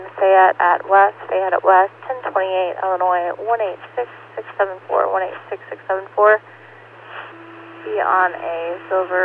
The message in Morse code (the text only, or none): none